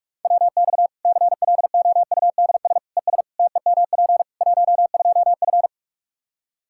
Morse code, heard as a keyed tone, about 41 wpm, 700 Hz. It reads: WX CLOUDS ES TEMP 12F
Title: Practice QSO - Exchange 2 - Info - 45wpm